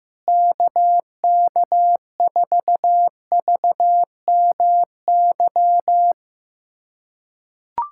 Morse code, heard as a keyed tone, about 15 wpm, 700 Hz. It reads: KK4VMY
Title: USA Callsign DE Callsign 15wpm